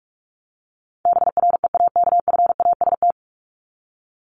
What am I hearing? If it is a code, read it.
Morse code, heard as a keyed tone, about 44 wpm, 700 Hz. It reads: BREAKFAST
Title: Top 1251-1500 Words 45wpm